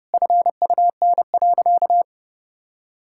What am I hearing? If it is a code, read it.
Morse code, heard as a keyed tone, about 29 wpm, 700 Hz. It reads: FUN.
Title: Sentences from Top 1000 Words - Review Part 5 - 30wpm